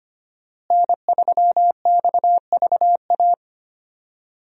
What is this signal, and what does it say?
Morse code, heard as a keyed tone, about 25 wpm, 700 Hz. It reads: N3XVA
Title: USA Call Signs - 5 Characters - Repeated 3x - 25wpm